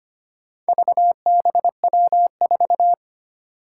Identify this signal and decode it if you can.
Morse code, heard as a keyed tone, about 25 wpm, 700 Hz. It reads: VBW4
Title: Four Challenging Letter-Number Combinations 25wpm